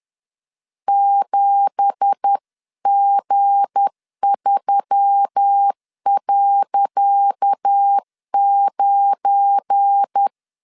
Morse code, heard as a keyed tone, about 10 wpm, 800 Hz. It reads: 7G3.9
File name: LEZIONI / 10 SIGNS / 10C DOT 10wpm 2940